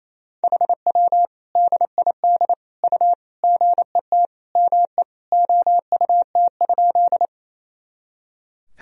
Morse code, heard as a keyed tone, about 28 wpm, 700 Hz. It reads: HW DID U GET ME OUT?
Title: Sentences from Top 200 Words and Abbreviations 28wpm